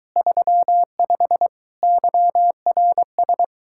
Morse code, heard as a keyed tone, about 23 wpm, 700 Hz. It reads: 35 YRS